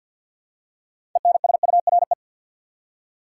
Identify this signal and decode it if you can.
Morse code, heard as a keyed tone, about 53 wpm, 700 Hz. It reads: ENSURE
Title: Six Letter Words - 50wpm